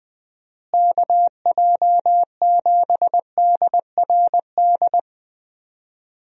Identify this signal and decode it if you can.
Morse code, heard as a keyed tone, about 20 wpm, 700 Hz. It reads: KJ7DRD
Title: USA Call Signs - 6 Characters - Repeated 2x - 20wpm